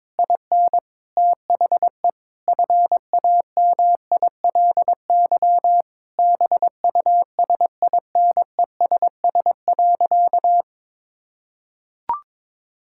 Morse code, heard as a keyed tone, about 22 wpm, 700 Hz. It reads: IN THE FAMILY BUSINESS.
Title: Sentences from Top 1000 Words - Review Part 5 - 22wpm